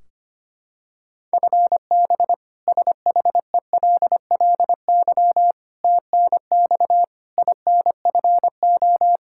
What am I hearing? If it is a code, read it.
Morse code, heard as a keyed tone, about 25 wpm, 700 Hz. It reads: FB SHELLY TNX INFO
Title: K1USN SST QSO Practice 25wpm